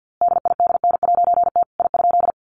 Morse code, heard as a keyed tone, about 50 wpm, 700 Hz. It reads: DIDN'T I?
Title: A Sound Of Thunder 50wpm